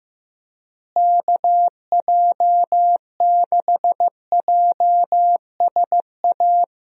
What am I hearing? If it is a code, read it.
Morse code, heard as a keyed tone, about 15 wpm, 700 Hz. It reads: KJ6JSA
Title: USA Call Signs - 6 Characters - Repeated 3x - 15wpm